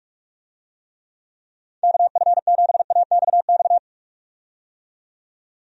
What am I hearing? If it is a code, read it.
Morse code, heard as a keyed tone, about 45 wpm, 700 Hz. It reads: KF7AXX
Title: USA Call Signs - 6 Characters - 45wpm